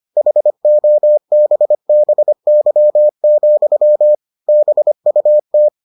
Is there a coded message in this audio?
Yes: Morse code from 0.2 to 5.7 s, heard as a keyed tone, about 25 wpm, 600 Hz. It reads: HOBBY, BUT